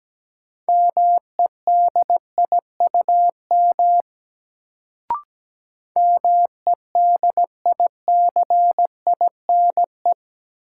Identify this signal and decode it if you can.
Morse code, heard as a keyed tone, about 17 wpm, 700 Hz. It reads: MEDIUM MEDICINE
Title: Sentences from Top 900 Words - New Word Drill - 17wpm